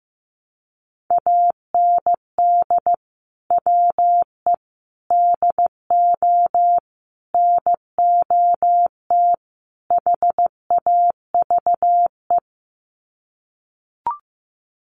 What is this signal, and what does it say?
Morse code, heard as a keyed tone, about 15 wpm, 700 Hz. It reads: AND WE DO NOT HAVE
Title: Sets of 5 Words Encore 15wpm